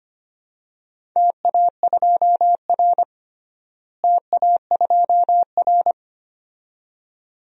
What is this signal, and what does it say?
Morse code, heard as a keyed tone, about 25 wpm, 700 Hz. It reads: TA2R TA2R
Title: DXCC-100 Call Signs - 4 Characters - Repeated 2x - 25wpm